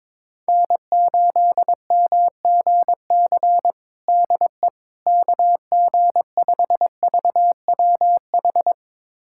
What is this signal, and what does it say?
Morse code, heard as a keyed tone, about 22 wpm, 700 Hz. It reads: N8MGC DE KG5VWH
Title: USA Callsign DE Callsign 22wpm